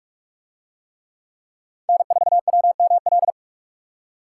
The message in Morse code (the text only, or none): N4WML